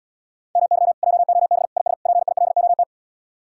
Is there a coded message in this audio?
Yes: Morse code from 0.6 to 2.8 s, heard as a keyed tone, about 77 wpm, 700 Hz. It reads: MY JOB IS PILOT